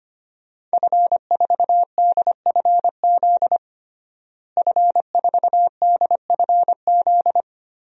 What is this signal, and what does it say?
Morse code, heard as a keyed tone, about 25 wpm, 700 Hz. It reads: F4DFZ F4DFZ
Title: DXCC-100 Call Signs - 5 Characters - Repeated 2x - 25wpm